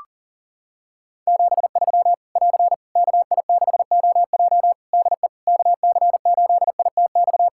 Morse code, heard as a keyed tone, about 40 wpm, 700 Hz. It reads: 73 <AR> KI6OJ DE KC8ITX
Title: K1USN SST QSO Practice 40wpm